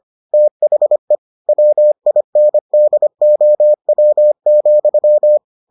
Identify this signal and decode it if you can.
Morse code, heard as a keyed tone, about 25 wpm, 600 Hz. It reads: THE WINDOW,